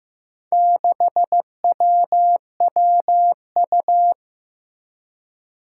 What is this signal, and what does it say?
Morse code, heard as a keyed tone, about 15 wpm, 700 Hz. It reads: 6WWU
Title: Four Challenging Letter-Number Combinations 15wpm